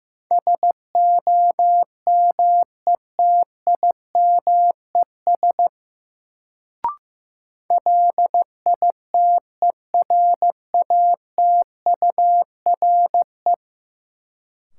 Morse code, heard as a keyed tone, about 15 wpm, 700 Hz. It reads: SOMETIMES LITERATURE
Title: Sentences from Top 600 Words - New Word Drill - 15wpm